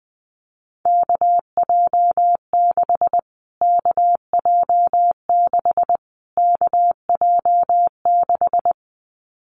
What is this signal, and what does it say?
Morse code, heard as a keyed tone, about 20 wpm, 700 Hz. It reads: KJ6 KJ6 KJ6
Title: Callsign Prefixes for Top 25 DXCC - Repeated 3x - 20wpm